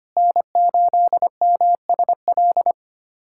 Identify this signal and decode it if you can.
Morse code, heard as a keyed tone, about 25 wpm, 700 Hz. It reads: N8MSL
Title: USA Callsign DE Callsign - 5 Characters - 25wpm